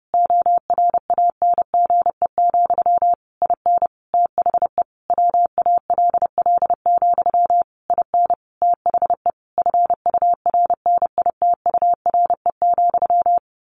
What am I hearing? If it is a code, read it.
Morse code, heard as a keyed tone, about 30 wpm, 700 Hz. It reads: ORANGE, IN THE WALL, IN THE FURNITURE,
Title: A Sound Of Thunder 30wpm